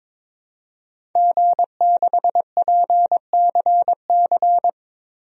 Morse code, heard as a keyed tone, about 22 wpm, 700 Hz. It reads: G6PCC
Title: DXCC-100 Call Signs - 5 Characters - 22wpm